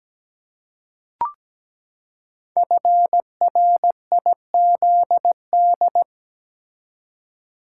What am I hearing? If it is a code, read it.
Morse code, heard as a keyed tone, about 17 wpm, 700 Hz. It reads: FRIZD
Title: Code Groups - 5 Letter:Numbers - 17wpm